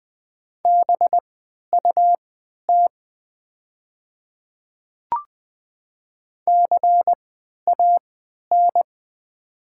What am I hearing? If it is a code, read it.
Morse code, heard as a keyed tone, about 20 wpm, spaced as for 11 wpm, 700 Hz. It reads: BUT CAN
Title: 3x Character Spacing - Top 100 Words - 20wpm